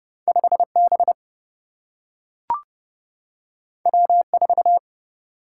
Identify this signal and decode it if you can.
Morse code, heard as a keyed tone, about 30 wpm, 700 Hz. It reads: 5 B W 4
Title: Single-Letter Single-Number 30wpm